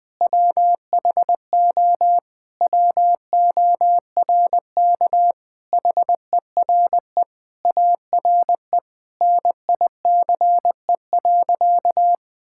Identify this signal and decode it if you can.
Morse code, heard as a keyed tone, about 20 wpm, 700 Hz. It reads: WHO WORK HERE ARE NICE.
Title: Sentences with 3-Word Phrases - 20wpm